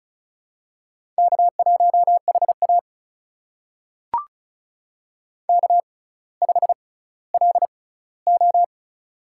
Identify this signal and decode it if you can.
Morse code, heard as a keyed tone, about 36 wpm, spaced as for 18 wpm, 700 Hz. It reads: K1HA K5LO
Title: USA Call Signs - 4 Characters - 6x Character Spacing - 35wpm